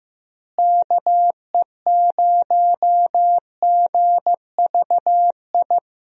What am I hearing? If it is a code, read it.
Morse code, heard as a keyed tone, about 15 wpm, 700 Hz. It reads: KE0GVI
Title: USA Call Signs - 6 Characters - Repeated 2x - 15wpm